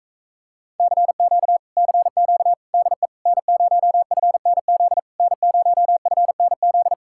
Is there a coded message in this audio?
Yes: Morse code from 0.8 to 6.9 s, heard as a keyed tone, about 45 wpm, 700 Hz. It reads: CQ CQ DE N0FNZ N0FNZ